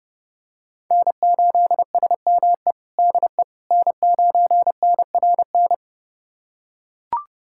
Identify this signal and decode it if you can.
Morse code, heard as a keyed tone, about 30 wpm, 700 Hz. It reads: N8SME DE N9NRN
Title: USA Callsign DE Callsign - 5 Characters - 30wpm